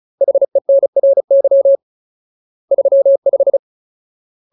Morse code, heard as a keyed tone, about 34 wpm, 550 Hz. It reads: HENRY 35
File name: Ultimate Word List / 35WPM Ultimate